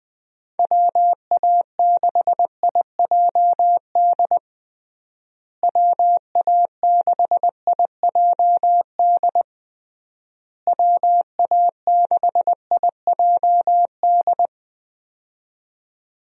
Morse code, heard as a keyed tone, about 20 wpm, 700 Hz. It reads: WA6IJD WA6IJD WA6IJD
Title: USA Call Signs - 6 Characters - Repeated 3x - 20wpm